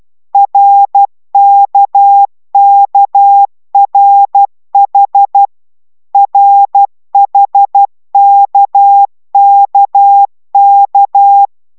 Morse code, heard as a keyed tone, about 12 wpm, 800 Hz. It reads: RKKRH RHKKK